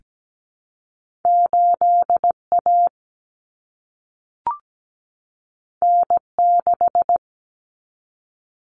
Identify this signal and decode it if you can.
Morse code, heard as a keyed tone, about 17 wpm, 700 Hz. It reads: 8A N6
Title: Common Cut Numbers - 2-Digit 17wpm